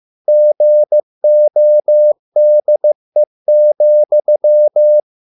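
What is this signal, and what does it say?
Morse code, heard as a keyed tone, about 15 wpm, 600 Hz. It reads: GODE,
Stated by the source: Morse code at a slow 15 wpm from CW med Gnister.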